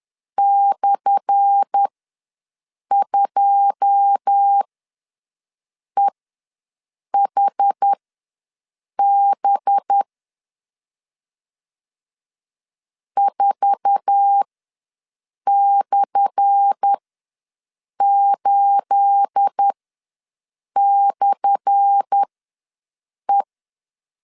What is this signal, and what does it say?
Morse code, heard as a keyed tone, about 10 wpm, spaced as for 6 wpm, 800 Hz. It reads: /2EHB 4/8/E
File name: LEZIONI / 10 SIGNS / 10A SLASH ws 10wpm 4121